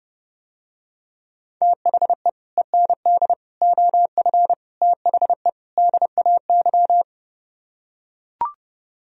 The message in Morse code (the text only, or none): THE END OF THE DAY